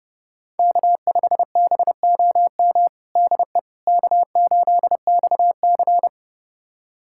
Morse code, heard as a keyed tone, about 32 wpm, 690 Hz. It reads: K5BOM DE K8XC
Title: USA Callsign DE Callsign 30wpm